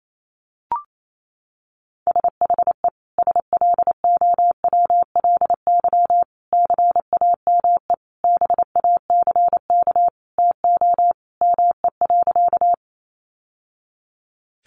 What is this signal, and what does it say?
Morse code, heard as a keyed tone, about 28 wpm, 700 Hz. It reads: SHE SLOWLY CAME BACK TO ME.
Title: Sentences from Top 1100 Words 28wpm